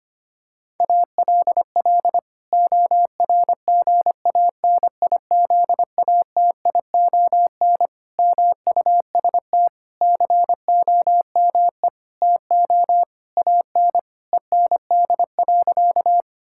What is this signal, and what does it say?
Morse code, heard as a keyed tone, about 25 wpm, 700 Hz. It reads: ALL ORGANIZATION MUST COME TO AN END.
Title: Sentences from Top 1000 Words - Review Part 5 - 25wpm